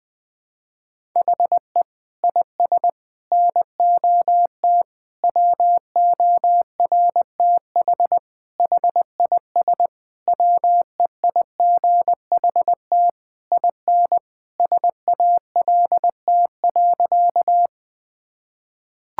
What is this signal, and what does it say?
Morse code, heard as a keyed tone, about 20 wpm, 700 Hz. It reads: HE IS NOT WORTH HIS WEIGHT IN SALT.
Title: Sentences from Top 1000 Words - Review Part 5 - 20wpm